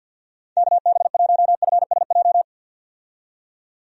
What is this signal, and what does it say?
Morse code, heard as a keyed tone, about 51 wpm, 700 Hz. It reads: KB1FSJ
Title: USA Call Signs - 6 Characters - 50wpm